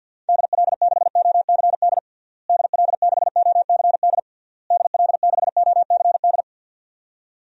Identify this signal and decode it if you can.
Morse code, heard as a keyed tone, about 52 wpm, 700 Hz. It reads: DL6OCD DL6OCD DL6OCD
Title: DXCC-100 Call Signs - 6 Characters - Repeated 3x - 50wpm